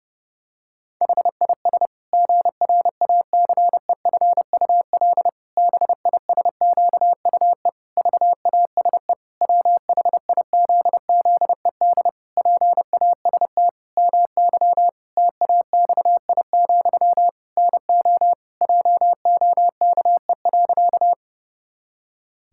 Morse code, heard as a keyed tone, about 28 wpm, 700 Hz. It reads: HIS GRACEFUL BISQUE VASE WHIZZED PAST MY TAXI, NO JOKE.
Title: Pangram Challenge v2 - 30wpm